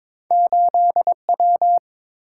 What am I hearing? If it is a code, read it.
Morse code, heard as a keyed tone, about 30 wpm, 700 Hz. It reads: 8 W